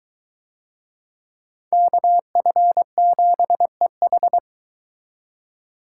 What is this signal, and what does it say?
Morse code, heard as a keyed tone, about 22 wpm, 700 Hz. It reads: KF7EH